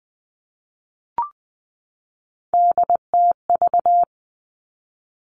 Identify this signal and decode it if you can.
Morse code, heard as a keyed tone, about 20 wpm, 700 Hz. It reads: DTV
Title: Three Challenging Letter-Number Combinations 20wpm